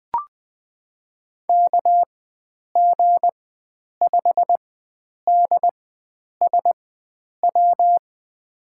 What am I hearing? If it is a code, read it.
Morse code, heard as a keyed tone, about 20 wpm, spaced as for 9 wpm, 700 Hz. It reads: KG5DSW
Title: USA Call Signs - 4x Character Spacing - 20wpm